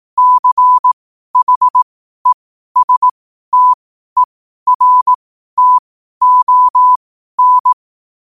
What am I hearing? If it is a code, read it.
Morse code, heard as a keyed tone, about 18 wpm, spaced as for 12 wpm, 1000 Hz. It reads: CHESTERTON
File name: qotdfiles / 12wpm